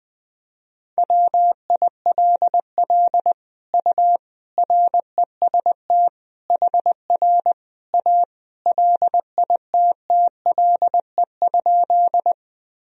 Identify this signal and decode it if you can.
Morse code, heard as a keyed tone, about 20 wpm, 700 Hz. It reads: WILL U REST HR A LITTLE?
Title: Sentences from Top 500 Words and Abbreviations 20wpm